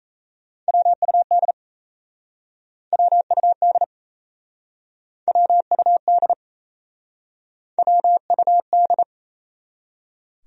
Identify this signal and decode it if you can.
Morse code, heard as a keyed tone, about 37 wpm, 700 Hz. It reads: WUD WUD WUD WUD